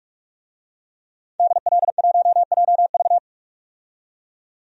Morse code, heard as a keyed tone, about 43 wpm, 700 Hz. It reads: DL1JV
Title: DXCC-100 Call Signs - 5 Characters - 45wpm